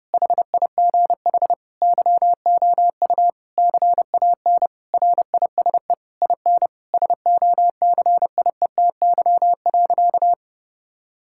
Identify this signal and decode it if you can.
Morse code, heard as a keyed tone, about 30 wpm, 700 Hz. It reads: HIGH YOU CAN RISE IN SOCIETY.
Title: Sentences from Top 1000 Words - Review Part 6 - 30wpm